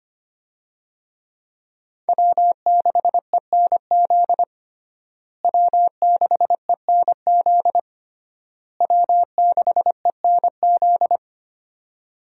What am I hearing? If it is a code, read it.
Morse code, heard as a keyed tone, about 25 wpm, 700 Hz. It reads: W6ENZ W6ENZ W6ENZ
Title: USA Call Signs - 5 Characters - Repeated 3x - 25wpm